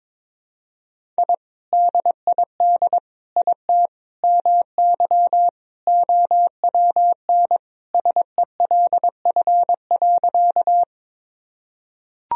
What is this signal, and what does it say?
Morse code, heard as a keyed tone, about 22 wpm, 700 Hz. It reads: I DID IT MY OWN SELF.
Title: Sentences from Top 1000 Words - Review Part 1 - 22wpm